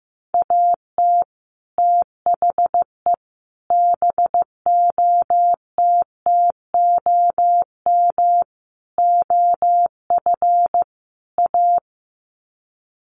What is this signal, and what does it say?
Morse code, heard as a keyed tone, about 15 wpm, 700 Hz. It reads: AT THE BOTTOM OF A
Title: Sets of 5 Words Encore2 15wpm